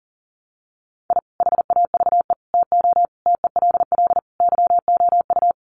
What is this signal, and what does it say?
Morse code, heard as a keyed tone, about 40 wpm, 700 Hz. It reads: I HAVE TO TELL YOU
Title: Sets of 5 Words 40wpm